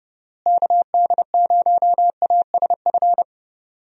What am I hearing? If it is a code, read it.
Morse code, heard as a keyed tone, about 30 wpm, 700 Hz. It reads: KD0ASF
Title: USA Call Signs - 6 Characters - Repeated 3x - 30wpm